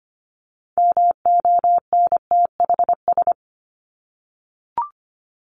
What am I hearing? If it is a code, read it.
Morse code, heard as a keyed tone, about 25 wpm, 700 Hz. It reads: MONTHS
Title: ES:S Suffix - Top 100 Words with Base Words 25wpm